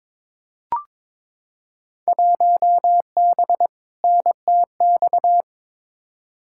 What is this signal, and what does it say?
Morse code, heard as a keyed tone, about 22 wpm, 700 Hz. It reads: 1B NTX
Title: Field Day Exchanges - 22wpm